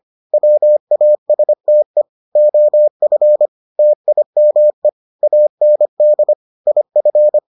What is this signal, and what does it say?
Morse code, heard as a keyed tone, about 25 wpm, 600 Hz. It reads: WASTE OF TIME AND IF